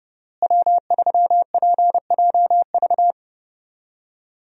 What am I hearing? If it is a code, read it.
Morse code, heard as a keyed tone, about 31 wpm, 700 Hz. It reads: W3PJV
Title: USA Call Signs - 5 Characters - Repeated 3x - 30wpm